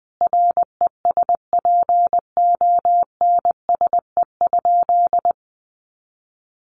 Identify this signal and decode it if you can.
Morse code, heard as a keyed tone, about 20 wpm, 700 Hz. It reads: RESPONSE?
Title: Sentences from Top 1400 Words 20wpm